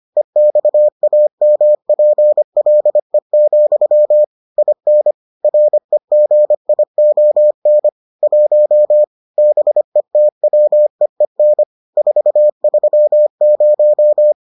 Morse code, heard as a keyed tone, about 25 wpm, 600 Hz. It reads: EXAMPLE, IN REGION 1 BETWEEN 430